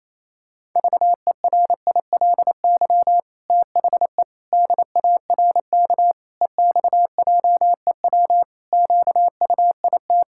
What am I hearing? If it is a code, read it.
Morse code, heard as a keyed tone, about 28 wpm, 700 Hz. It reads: VERILY THE DARK EXJEW QUIT
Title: Pangram Challenge v2 - 28wpm